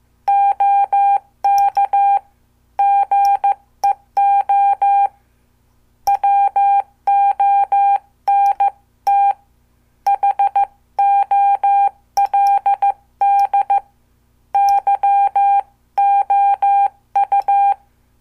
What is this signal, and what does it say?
Morse code, heard as a keyed tone, about 14 wpm, 800 Hz. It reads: OK GEO WONT HOLD YOU
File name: morse / qso